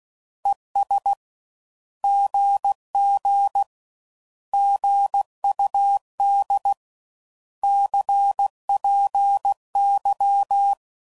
Morse code, heard as a keyed tone, about 16 wpm, 800 Hz. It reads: ES GG GUD CPY
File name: Mixed / Misc / Words